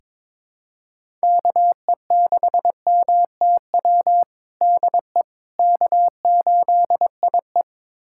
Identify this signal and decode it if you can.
Morse code, heard as a keyed tone, about 21 wpm, 700 Hz. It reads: KE6MTW DE K8IE
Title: USA Callsign DE Callsign 22wpm